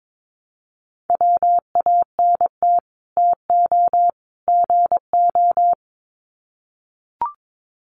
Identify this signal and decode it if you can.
Morse code, heard as a keyed tone, about 22 wpm, 700 Hz. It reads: WANT TO GO
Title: Sets of 3 Words 22wpm